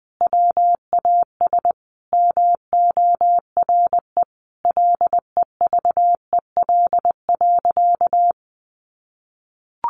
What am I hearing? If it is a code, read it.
Morse code, heard as a keyed tone, about 20 wpm, 700 Hz. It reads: WAS MORE LEVEL.
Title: Sentences from Top 700 Words 20wpm